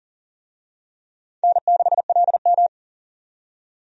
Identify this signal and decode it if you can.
Morse code, heard as a keyed tone, about 40 wpm, 700 Hz. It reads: N6LM
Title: USA Call Signs - 4 Characters - 40wpm